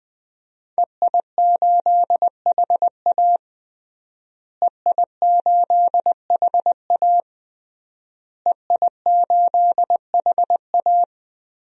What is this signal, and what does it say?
Morse code, heard as a keyed tone, about 20 wpm, 700 Hz. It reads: EI8HA EI8HA EI8HA
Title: DXCC-100 Call Signs - 5 Characters - Repeated 3x - 20wpm